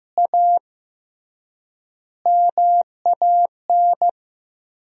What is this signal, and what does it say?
Morse code, heard as a keyed tone, about 15 wpm, 700 Hz. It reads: A MAN
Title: 3x Word-Spacing - Sentences from Top 200 Words - 15wpm